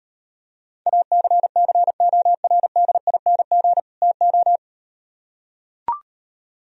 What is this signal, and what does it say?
Morse code, heard as a keyed tone, about 38 wpm, 700 Hz. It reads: ACCORDING TO